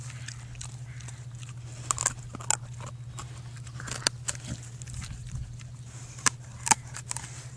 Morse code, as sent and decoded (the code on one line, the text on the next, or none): none